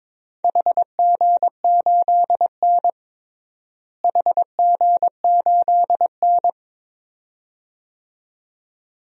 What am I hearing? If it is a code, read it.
Morse code, heard as a keyed tone, about 22 wpm, 700 Hz. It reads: HG8N HG8N
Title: DXCC-100 Call Signs - 4 Characters - Repeated 2x - 22wpm